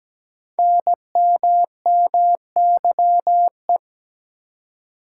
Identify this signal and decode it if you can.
Morse code, heard as a keyed tone, about 17 wpm, 700 Hz. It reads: NMMYE
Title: Code Groups - 5 Letters - 17wpm